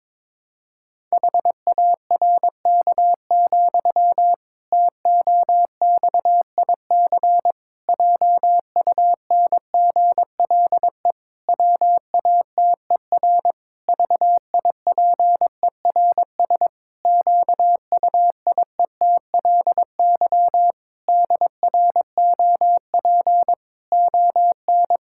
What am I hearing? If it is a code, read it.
Morse code, heard as a keyed tone, about 22 wpm, 700 Hz. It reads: HARK, TOXIC JUNGLE WATER VIPERS QUIETLY DROP ON
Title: Pangram Challenge v2 - 22wpm